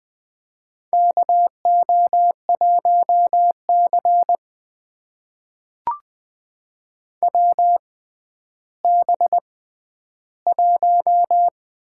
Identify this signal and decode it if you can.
Morse code, heard as a keyed tone, about 20 wpm, spaced as for 11 wpm, 700 Hz. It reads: KO1C WB1
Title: USA Call Signs - 4 Characters - 6x Character Spacing - 20wpm